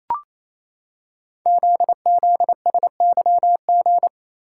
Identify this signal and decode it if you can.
Morse code, heard as a keyed tone, about 27 wpm, 700 Hz. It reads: ZZSYG
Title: Code Groups - 5 Letters - 28wpm